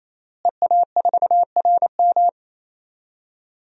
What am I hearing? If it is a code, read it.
Morse code, heard as a keyed tone, about 28 wpm, 700 Hz. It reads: EA4RM
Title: DXCC-100 Call Signs - 5 Characters - Repeated 3x - 28wpm